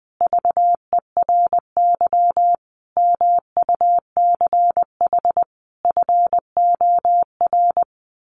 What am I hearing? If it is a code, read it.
Morse code, heard as a keyed tone, about 20 wpm, 700 Hz. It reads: VERY MUCH FOR
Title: Sets of 5 Words 20wpm